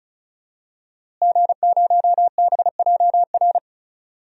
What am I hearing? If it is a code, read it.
Morse code, heard as a keyed tone, about 34 wpm, 700 Hz. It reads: G0BJR
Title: DXCC-100 Call Signs - 5 Characters - Repeated 2x - 35wpm